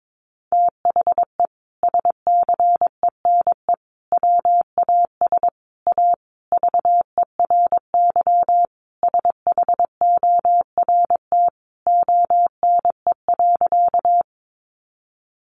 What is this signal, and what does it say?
Morse code, heard as a keyed tone, about 22 wpm, 700 Hz. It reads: THE SCENE WAS A VERY SHORT ONE.
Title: Sentences from Top 1100 Words 22wpm